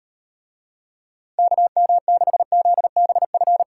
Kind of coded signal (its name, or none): Morse code